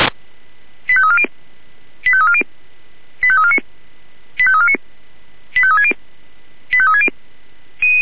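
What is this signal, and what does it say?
Five-tone selective call (ZVEI), heard as quick runs of five tones, tones of about 70 ms. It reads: 85340 85340 85349 85349 85360 85360
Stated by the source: ZVEI five-tone call 85360 received on an SDR.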